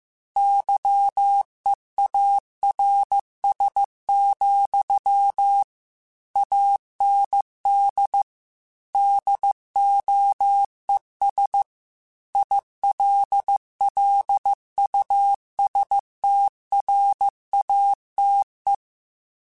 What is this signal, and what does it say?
Morse code, heard as a keyed tone, about 15 wpm, 800 Hz. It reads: YEARS, AND DOES ILLUSTRATE
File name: Technical Subjects / Technical Subjects-mp3s / AND Gate @13